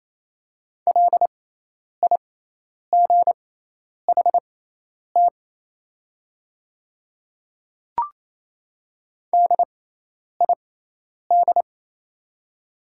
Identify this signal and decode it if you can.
Morse code, heard as a keyed tone, about 28 wpm, spaced as for 10 wpm, 700 Hz. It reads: LIGHT DID
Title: 6x Character Spacing - Top 200 Words - 28wpm